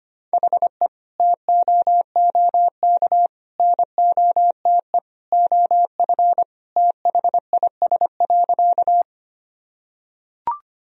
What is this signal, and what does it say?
Morse code, heard as a keyed tone, about 25 wpm, 700 Hz. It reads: HE TOOK NOTE OF THIS.
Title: Sentences from Top 1000 Words - Review Part 2 - 25wpm